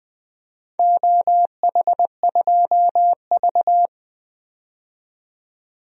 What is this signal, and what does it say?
Morse code, heard as a keyed tone, about 20 wpm, 700 Hz. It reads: OH2V
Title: DXCC-100 Call Signs - 4 Characters - 20wpm